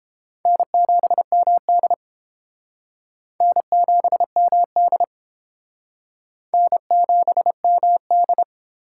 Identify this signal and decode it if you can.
Morse code, heard as a keyed tone, about 29 wpm, 700 Hz. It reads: N7MD N7MD N7MD